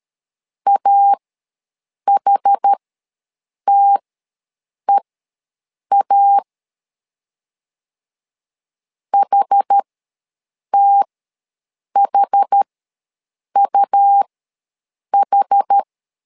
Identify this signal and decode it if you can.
Morse code, heard as a keyed tone, about 13 wpm, spaced as for 7 wpm, 800 Hz. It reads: AHTEA HTHUH
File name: LEZIONI / 2 UHDN / 2 UH ws 12wpm 1730